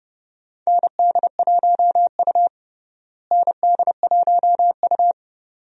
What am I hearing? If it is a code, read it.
Morse code, heard as a keyed tone, about 32 wpm, 700 Hz. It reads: ND1U ND1U
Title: USA Call Signs - 4 Characters - Repeated 2x - 30wpm